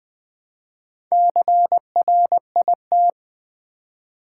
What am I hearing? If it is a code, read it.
Morse code, heard as a keyed tone, about 20 wpm, 700 Hz. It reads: CRIT
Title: Build A Word - 2 Letters At A Time - 8-Letter Words - 20wpm